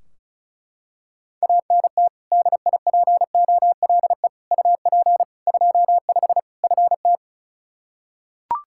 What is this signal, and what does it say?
Morse code, heard as a keyed tone, about 36 wpm, 700 Hz. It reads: ANT DIPOLE UP 25 FT
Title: Practice QSO - Exchange 2 - Info - 35wpm